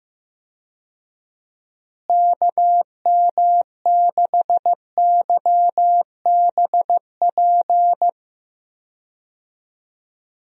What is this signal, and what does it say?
Morse code, heard as a keyed tone, about 15 wpm, 700 Hz. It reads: KM6YBP
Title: USA Call Signs - 6 Characters - Repeated 3x - 15wpm